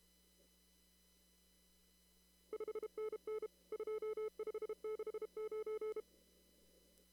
Morse code, heard as a keyed tone, about 32 wpm, 430 Hz. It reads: 5NN 2569
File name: CW / kt5j wpx cw-op-k5pi